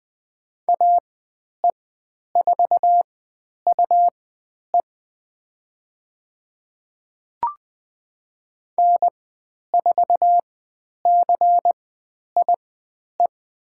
Morse code, heard as a keyed tone, about 20 wpm, spaced as for 10 wpm, 700 Hz. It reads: AE4UE N4CIE
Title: 109 Lesson 18g - Letter m - Call Signs 20wpm